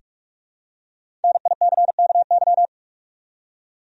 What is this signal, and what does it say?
Morse code, heard as a keyed tone, about 46 wpm, 700 Hz. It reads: NICKY
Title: USA Common Names - 45wpm